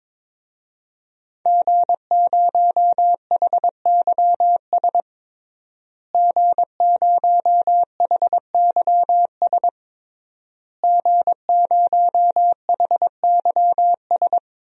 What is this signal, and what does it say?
Morse code, heard as a keyed tone, about 21 wpm, 700 Hz. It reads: G0HYS G0HYS G0HYS
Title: DXCC-100 Call Signs - 5 Characters - Repeated 3x - 22wpm